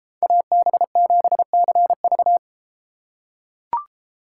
Morse code, heard as a keyed tone, about 33 wpm, 700 Hz. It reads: AB7CV